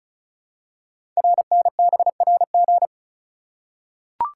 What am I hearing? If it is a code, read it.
Morse code, heard as a keyed tone, about 36 wpm, 700 Hz. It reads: RNBRG
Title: Code Groups - 5 Letters - 35wpm